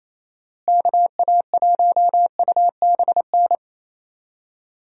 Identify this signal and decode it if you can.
Morse code, heard as a keyed tone, about 27 wpm, 700 Hz. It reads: KA1UBN
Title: USA Call Signs - 6 Characters - Repeated 2x - 28wpm